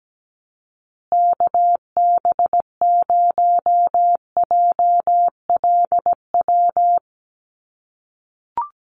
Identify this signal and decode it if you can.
Morse code, heard as a keyed tone, about 17 wpm, 700 Hz. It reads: KB0JLW
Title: USA Call Signs Repeated 2x 17wpm